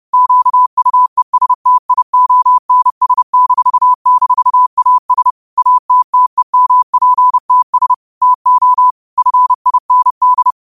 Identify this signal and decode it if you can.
Morse code, heard as a keyed tone, about 30 wpm, 1000 Hz. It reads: QUESTIONS--AS ATTEMPTS TO FIND
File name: qotdfiles / 30wpm